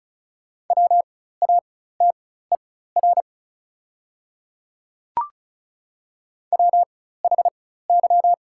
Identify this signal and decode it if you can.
Morse code, heard as a keyed tone, about 36 wpm, spaced as for 16 wpm, 700 Hz. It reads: WATER WHY
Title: 4x Character Spacing - Top 200 Words - 35wpm